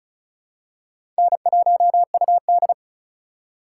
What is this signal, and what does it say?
Morse code, heard as a keyed tone, about 36 wpm, 700 Hz. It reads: N1UD
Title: USA Call Signs - 4 Characters - 3x Character Spacing - 35wpm